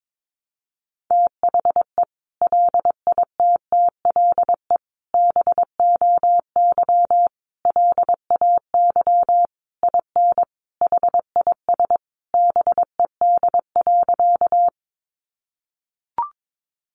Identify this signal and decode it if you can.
Morse code, heard as a keyed tone, about 22 wpm, 700 Hz. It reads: THE LITTLE BOY LAY IN HIS BED.
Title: Sentences from Top 500 Words 22wpm